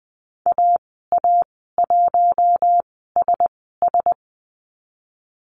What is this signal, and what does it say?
Morse code, heard as a keyed tone, about 20 wpm, spaced as for 15 wpm, 700 Hz. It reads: AA1SS
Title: 055 Lesson 10g - Letter r - Call Signs 20wpm